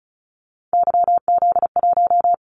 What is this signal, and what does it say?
Morse code, heard as a keyed tone, about 36 wpm, 690 Hz. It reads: YZ1
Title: Callsign Prefixes for Top 25 DXCC - 35wpm